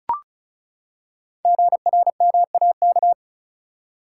Morse code, heard as a keyed tone, about 36 wpm, 690 Hz. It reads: GRMAK
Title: Code Groups - 5 Letters - 35wpm